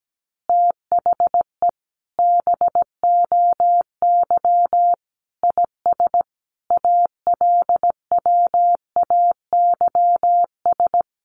Morse code, heard as a keyed tone, about 17 wpm, 700 Hz. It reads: THE BOY IS ALWAYS
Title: Additional Easy Phrases 17wpm